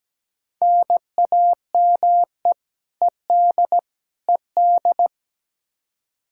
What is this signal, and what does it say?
Morse code, heard as a keyed tone, about 17 wpm, 700 Hz. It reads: NAME ED ED
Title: Practice QSO - Exchange 1 - Beginning - 17wpm